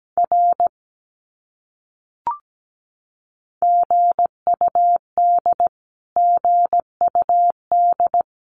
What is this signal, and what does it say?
Morse code, heard as a keyed tone, about 17 wpm, 700 Hz. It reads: R GUD GUD
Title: Abbreviations Repeated 3x - 17wpm